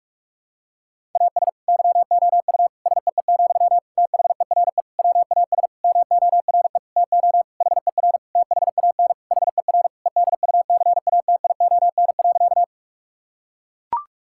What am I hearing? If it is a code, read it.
Morse code, heard as a keyed tone, about 41 wpm, 700 Hz. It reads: AS YOU SEE, THERE WAS MORE TO HER THAN HER EDUCATION.
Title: Sentences from Top 1000 Words - Review Part 6 - 45wpm